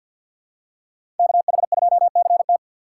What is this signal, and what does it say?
Morse code, heard as a keyed tone, about 53 wpm, 700 Hz. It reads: KH2CT
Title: USA Call Signs - 4x Character Spacing - 50wpm